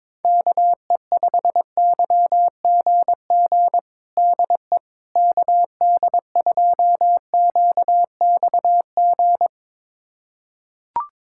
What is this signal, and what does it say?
Morse code, heard as a keyed tone, about 22 wpm, 700 Hz. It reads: KE5YGG DE KD2QXG
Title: USA Callsign DE Callsign - 6 Characters - 22wpm